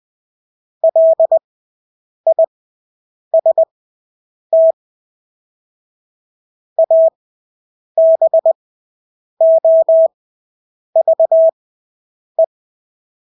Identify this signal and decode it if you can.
Morse code, heard as a keyed tone, about 20 wpm, spaced as for 8 wpm, 650 Hz. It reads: LIST ABOVE
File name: Engelske ord 20/8 wpm